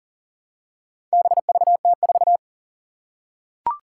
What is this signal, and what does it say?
Morse code, heard as a keyed tone, about 40 wpm, 700 Hz. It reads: BVT4
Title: Four Challenging Letter-Number Combinations 40wpm